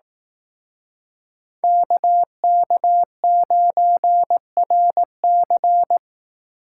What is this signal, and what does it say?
Morse code, heard as a keyed tone, about 18 wpm, 700 Hz. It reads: KK9RC
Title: Speed Racing - USA 2x2 Call Signs 15wpm